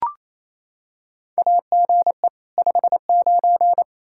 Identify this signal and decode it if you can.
Morse code, heard as a keyed tone, about 28 wpm, 700 Hz. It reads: AGE 59
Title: Practice QSO - Exchange 3 - More Info - 28wpm